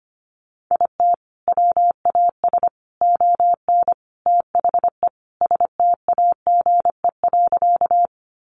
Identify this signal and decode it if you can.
Morse code, heard as a keyed tone, about 25 wpm, 700 Hz. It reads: IT WAS ON THE STAGE.
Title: Sentences from Top 1200 Words 25wpm